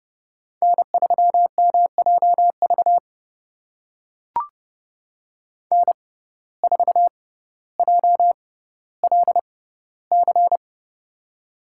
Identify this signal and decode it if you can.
Morse code, heard as a keyed tone, about 30 wpm, spaced as for 15 wpm, 700 Hz. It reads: N3MJV N4JLC
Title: USA Call Signs - 5 Characters - 6x Character Spacing - 30wpm